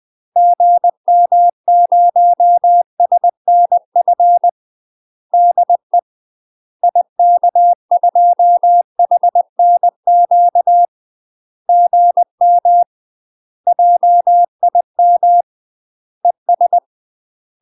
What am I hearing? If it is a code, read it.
Morse code, heard as a keyed tone, about 20 wpm, 700 Hz. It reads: GM0SNF DE IK2HNQ GM JIM ES
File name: UK MASTER QSOs TRAINING / 20WPM / UK MASTER QSOs 0060